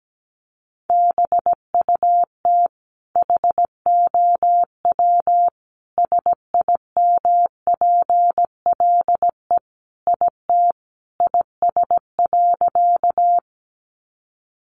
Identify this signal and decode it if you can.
Morse code, heard as a keyed tone, about 17 wpm, 700 Hz. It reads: BUT HOW SIMPLE IT IS.
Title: Sentences from Top 400 Words 17wpm